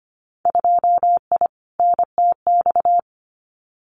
Morse code, heard as a keyed tone, about 25 wpm, 700 Hz. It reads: 2I NTX
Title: Winter Field Day Exchanges Encore 25wpm